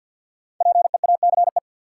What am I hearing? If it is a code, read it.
Morse code, heard as a keyed tone, about 52 wpm, 700 Hz. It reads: PEACE
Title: Sentences from Top 1500 Words - New Word Drill Alt - 50wpm